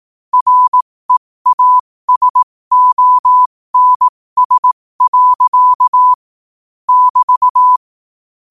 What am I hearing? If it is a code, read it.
Morse code, heard as a keyed tone, about 18 wpm, spaced as for 15 wpm, 1000 Hz. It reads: REASONS. <BT>
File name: qotdfiles / 15wpm